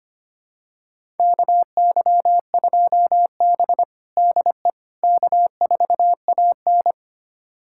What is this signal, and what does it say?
Morse code, heard as a keyed tone, about 25 wpm, 700 Hz. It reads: KY2B DE K4AN
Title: USA Callsign DE Callsign - 4 Characters - 25wpm